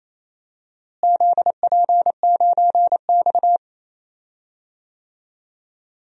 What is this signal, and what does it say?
Morse code, heard as a keyed tone, about 28 wpm, 700 Hz. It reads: ZP9X
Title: DXCC-100 Call Signs - 4 Characters - Repeated 2x - 28wpm